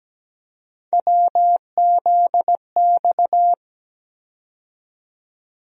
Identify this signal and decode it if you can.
Morse code, heard as a keyed tone, about 17 wpm, 700 Hz. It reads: WZX
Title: USA Call Sign Postfixes - 3 Letters 17wpm